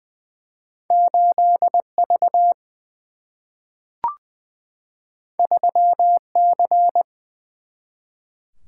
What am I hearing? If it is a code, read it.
Morse code, heard as a keyed tone, about 20 wpm, 700 Hz. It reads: 8 V 3 C
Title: Single-Letter Single-Number 20wpm